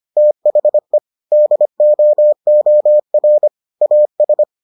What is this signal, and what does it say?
Morse code, heard as a keyed tone, about 25 wpm, 600 Hz. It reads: THE DOOR AS